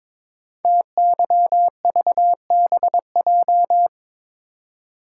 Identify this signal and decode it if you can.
Morse code, heard as a keyed tone, about 22 wpm, 700 Hz. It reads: TYVBJ
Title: Code Groups - 5 Letters - 22wpm